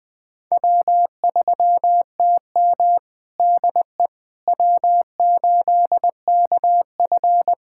Morse code, heard as a keyed tone, about 20 wpm, 700 Hz. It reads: W3TM DE W8KF
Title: USA Callsign DE Callsign - 4 Characters - 20wpm